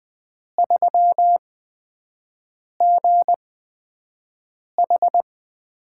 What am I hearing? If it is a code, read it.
Morse code, heard as a keyed tone, about 20 wpm, spaced as for 5 wpm, 700 Hz. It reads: 3GH
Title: USA Call Signs - 8x Character Spacing - 20wpm